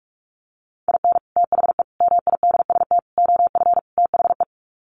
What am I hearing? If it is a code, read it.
Morse code, heard as a keyed tone, about 44 wpm, 700 Hz. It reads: IN THE MIDST OF THE
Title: Sets of 5 Words 45wpm